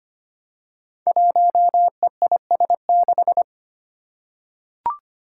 Morse code, heard as a keyed tone, about 25 wpm, 700 Hz. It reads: 1EIS6
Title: Five Challenging Letter-Number Combinations 25wpm